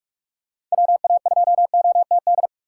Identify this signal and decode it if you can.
Morse code, heard as a keyed tone, about 46 wpm, 700 Hz. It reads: WA2OTD
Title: USA Call Signs - 6 Characters - 2x Character Spacing - 45wpm